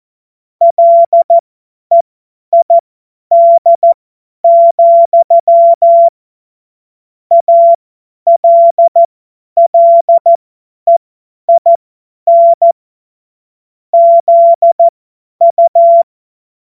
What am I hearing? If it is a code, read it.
Morse code, heard as a keyed tone, about 14 wpm, spaced as for 10 wpm, 680 Hz. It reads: LEID, ALLEIN ZU
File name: Morsekurs / Witze / Witz 23. Dezember 2025 50 F70